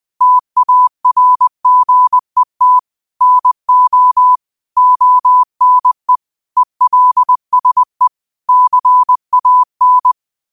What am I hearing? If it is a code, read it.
Morse code, heard as a keyed tone, about 20 wpm, 1000 Hz. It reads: TARGET NO ONE ELSE CAN